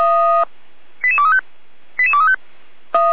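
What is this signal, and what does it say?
Five-tone selective call (ZVEI), heard as quick runs of five tones, tones of about 70 ms. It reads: 88235 88235
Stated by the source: ZVEI five-tone selective call 88267 received on an SDR.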